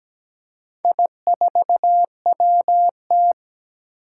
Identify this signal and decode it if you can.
Morse code, heard as a keyed tone, about 17 wpm, 700 Hz. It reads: I4WT
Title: Four Challenging Letter-Number Combinations 17wpm